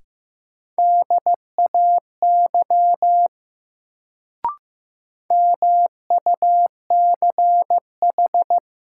Morse code, heard as a keyed tone, about 15 wpm, 700 Hz. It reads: DAY MUCH
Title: Sentences from Top 200 Words - New Word Drill - 15wpm